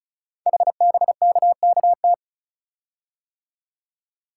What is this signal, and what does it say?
Morse code, heard as a keyed tone, about 36 wpm, 700 Hz. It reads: HBKKT
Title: Code Groups - 5 Letters - 35wpm